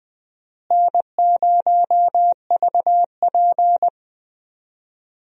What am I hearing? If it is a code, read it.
Morse code, heard as a keyed tone, about 20 wpm, 700 Hz. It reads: N0VP
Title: USA Call Signs - 4 Characters - Repeated 2x - 20wpm